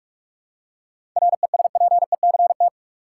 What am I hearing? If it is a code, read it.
Morse code, heard as a keyed tone, about 45 wpm, 700 Hz. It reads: RESPECT
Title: Seven Letter Words - 45wpm